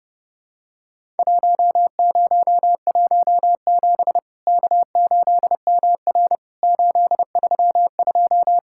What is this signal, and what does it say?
Morse code, heard as a keyed tone, about 29 wpm, 700 Hz. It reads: 1017 K8MR 832
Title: Worked All Europe DX Contest - QTC Exchange - 30wpm